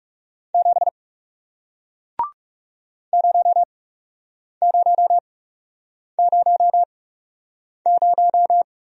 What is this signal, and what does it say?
Morse code, heard as a keyed tone, about 38 wpm, spaced as for 8 wpm, 700 Hz. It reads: 7 0000
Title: Speed-Racing Single Letter-Number 30wpm